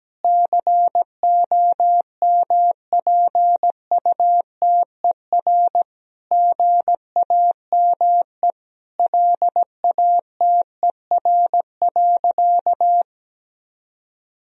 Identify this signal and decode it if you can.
Morse code, heard as a keyed tone, about 17 wpm, 700 Hz. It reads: COMPUTER GAME LATER.
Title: Sentences from Top 1000 Words - Review Part 3 - 17wpm